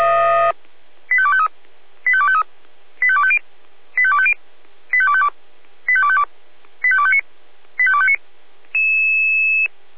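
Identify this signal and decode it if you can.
Five-tone selective call (ZVEI), heard as quick runs of five tones, tones of about 70 ms. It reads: 85253 85253 85250 85250 85252 85252 85259 85259
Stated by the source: ZVEI five-tone selective call 85358 received on SDR.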